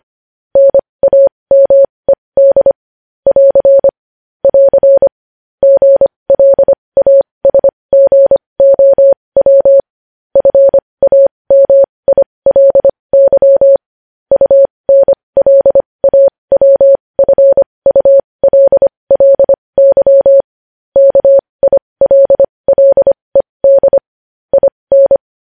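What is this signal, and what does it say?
Morse code, heard as a keyed tone, about 25 wpm, spaced as for 20 wpm, 550 Hz. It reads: NAMED <AR> <AR> GLASGOW FAMILY UNLAWFULLY KILLED IN
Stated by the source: Morse code news headlines practice at 20 wpm.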